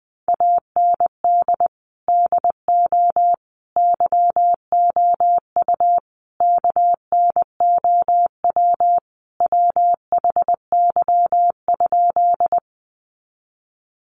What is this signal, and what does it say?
Morse code, heard as a keyed tone, about 20 wpm, 700 Hz. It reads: AND DO YOU KNOW WHY?
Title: Sentences from Top 200 Words 20wpm